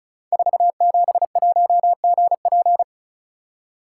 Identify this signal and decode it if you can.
Morse code, heard as a keyed tone, about 34 wpm, 700 Hz. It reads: 471GP
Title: Code Groups - 5 Letter:Numbers - 35wpm